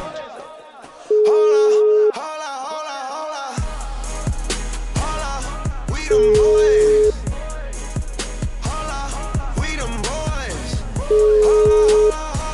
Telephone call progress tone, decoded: ringback tone

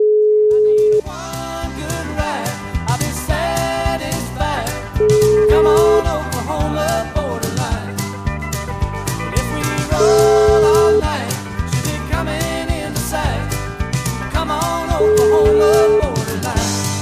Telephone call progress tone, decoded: ringback tone

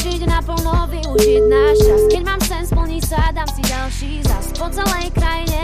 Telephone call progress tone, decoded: ringback tone

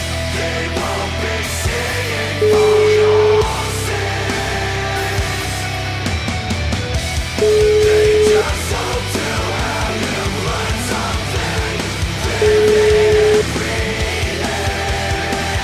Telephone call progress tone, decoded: ringback tone

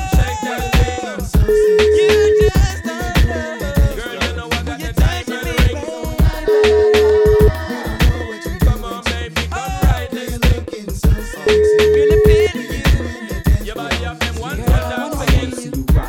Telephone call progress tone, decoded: ringback tone

